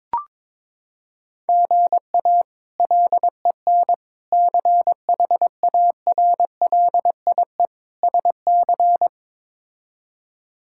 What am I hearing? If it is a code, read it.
Morse code, heard as a keyed tone, about 21 wpm, 700 Hz. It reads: GA LEN CHARLIE SC
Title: K1USN SST QSO Practice - 22wpm